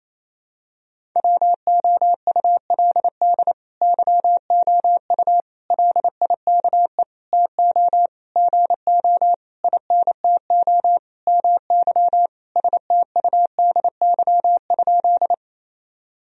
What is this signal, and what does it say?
Morse code, heard as a keyed tone, about 28 wpm, 700 Hz. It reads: WOULD YOU LIKE TO GO INTO MY STUDY?
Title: Sentences from Top 1000 Words - Review Part 1 - 28wpm